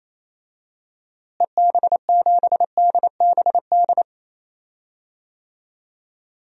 Morse code, heard as a keyed tone, about 28 wpm, 700 Hz. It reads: EB7DBD
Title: DXCC-100 Call Signs - 6 Characters - Repeated 3x - 28wpm